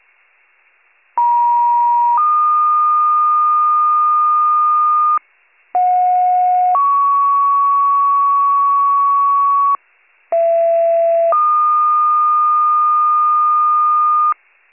Two-tone page, heard as a one-second tone, then a three-second tone, about 950 then 1220 Hz, about 730 then 1120 Hz, about 670 then 1220 Hz. It is heard from 1.2 to 14.3 s.